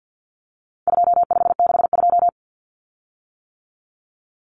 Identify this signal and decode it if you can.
Morse code, heard as a keyed tone, about 50 wpm, 700 Hz. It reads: 2562
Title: Common Cut Numbers - 4-Digit 50wpm